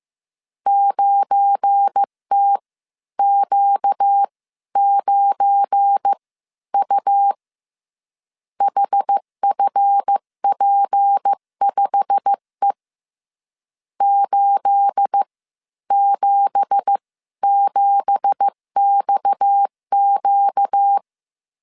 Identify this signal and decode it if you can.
Morse code, heard as a keyed tone, about 15 wpm, spaced as for 12 wpm, 800 Hz. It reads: 9TQ9U HFP5E 877XQ
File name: CW / DET / 1 9 1500C S B 14wpm 3302